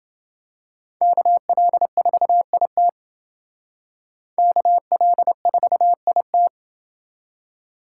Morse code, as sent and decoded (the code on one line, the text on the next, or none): -.- .-.. ....- .. - / -.- .-.. ....- .. -
KL4IT KL4IT